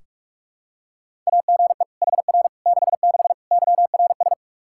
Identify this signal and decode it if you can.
Morse code, heard as a keyed tone, about 46 wpm, 700 Hz. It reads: AGE HR 66 YRS
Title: Practice QSO - Exchange 3 - More Info - 45wpm